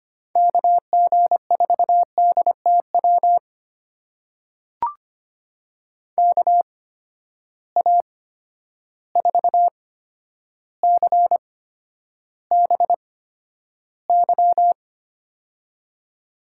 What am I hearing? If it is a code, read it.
Morse code, heard as a keyed tone, about 25 wpm, spaced as for 11 wpm, 700 Hz. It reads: KG4DTW KA4CBY
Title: USA Call Signs - 6 Characters - 8x Character Spacing - 25wpm